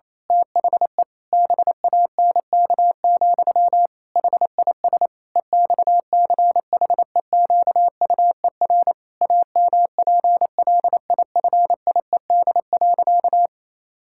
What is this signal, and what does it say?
Morse code, heard as a keyed tone, about 28 wpm, 700 Hz. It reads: THE BANK, HIS EXCHEQUER AMPLIFIED.
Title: Pangram Challenge v2 - 28wpm